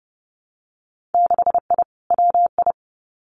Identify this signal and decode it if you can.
Morse code, heard as a keyed tone, about 30 wpm, 700 Hz. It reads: 6I WI
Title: Winter Field Day Exchanges Encore 30wpm